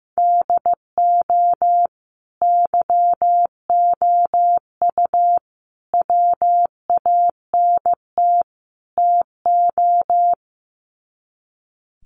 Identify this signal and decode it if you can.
Morse code, heard as a keyed tone, about 15 wpm, 700 Hz. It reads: DO YOU WANT TO
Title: Sets of 4 Words 15wpm